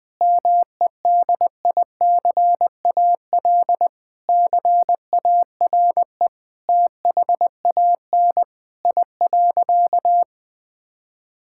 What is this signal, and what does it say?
Morse code, heard as a keyed tone, about 20 wpm, 700 Hz. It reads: MEDICAL CARE THAN I.
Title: Sentences from Top 1000 Words - Review Part 5 - 20wpm